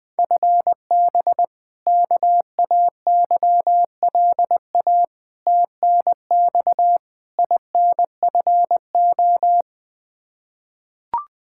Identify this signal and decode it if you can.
Morse code, heard as a keyed tone, about 20 wpm, 700 Hz. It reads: FB KAYLA TNX INFO
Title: K1USN SST QSO Practice 20wpm